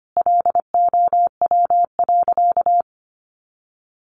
Morse code, heard as a keyed tone, about 25 wpm, 700 Hz. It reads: LOW.
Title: Sentences from Top 1500 Words 25wpm